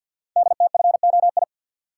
Morse code, heard as a keyed tone, about 52 wpm, 700 Hz. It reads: DTFOI
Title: Code Groups - 5 Letters - 50wpm